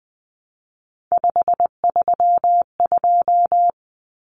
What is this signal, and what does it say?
Morse code, heard as a keyed tone, about 20 wpm, 700 Hz. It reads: 532
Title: Numbers - 3-Digit 20wpm